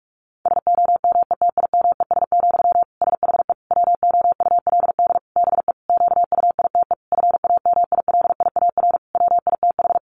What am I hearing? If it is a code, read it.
Morse code, heard as a keyed tone, about 44 wpm, 700 Hz. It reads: SOMETIMES, SHE WOULD BE QUITE FAMILIAR WITH
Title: Sentences from Top 1000 Words 45wpm